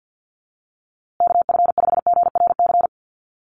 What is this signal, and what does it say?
Morse code, heard as a keyed tone, about 50 wpm, 700 Hz. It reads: KF5GRC
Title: USA Call Signs Repeated 2x 50wpm